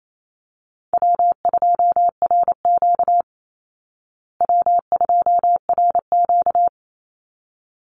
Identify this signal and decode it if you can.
Morse code, heard as a keyed tone, about 28 wpm, 700 Hz. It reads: W2RQ W2RQ
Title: Easy Call Signs Repeated 2x - 28wpm